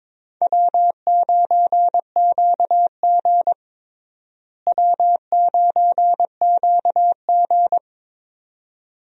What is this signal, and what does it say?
Morse code, heard as a keyed tone, about 22 wpm, 700 Hz. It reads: W9QG W9QG
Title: USA Call Signs - 4 Characters - Repeated 2x - 22wpm